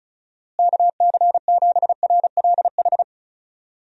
Morse code, heard as a keyed tone, about 36 wpm, 700 Hz. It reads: KC7RLH
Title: USA Call Signs - 6 Characters - Repeated 3x - 35wpm